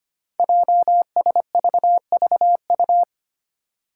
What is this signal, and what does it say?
Morse code, heard as a keyed tone, about 25 wpm, 700 Hz. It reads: JSVVU
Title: Five Challenging Letter-Number Combinations 25wpm